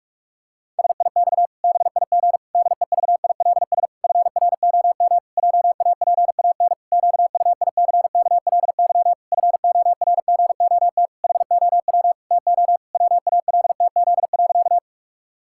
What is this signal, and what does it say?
Morse code, heard as a keyed tone, about 44 wpm, 700 Hz. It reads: SIX BIG DEVILS FROM JAPAN QUICKLY FORGOT HOW TO WALTZ.
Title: Pangram Challenge v2 - 45wpm